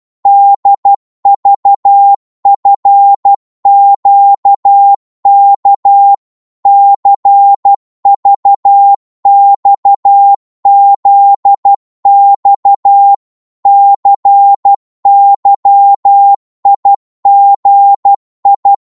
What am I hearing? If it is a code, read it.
Morse code, heard as a keyed tone, about 12 wpm, 800 Hz. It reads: DVFQK CVXZX CYIGI
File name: Morse / ltr-12wpm